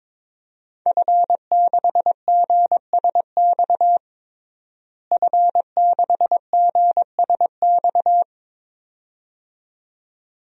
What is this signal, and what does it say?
Morse code, heard as a keyed tone, about 22 wpm, 700 Hz. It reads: F6GSX F6GSX
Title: DXCC-100 Call Signs - 5 Characters - Repeated 2x - 22wpm